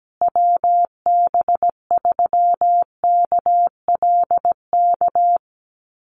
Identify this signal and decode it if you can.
Morse code, heard as a keyed tone, about 17 wpm, 700 Hz. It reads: WB3KLK
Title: USA Call Signs Repeated 3x 17wpm